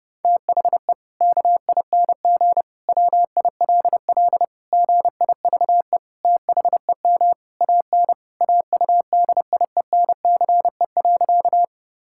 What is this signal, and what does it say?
Morse code, heard as a keyed tone, about 30 wpm, 700 Hz. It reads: THE KING WILL GIVE THEM AN AUDIENCE.
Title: Sentences from Top 1000 Words - Review Part 5 - 30wpm